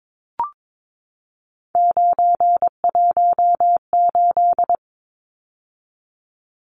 Morse code, heard as a keyed tone, about 22 wpm, 700 Hz. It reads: 918
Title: Numbers - 3-Digit 22wpm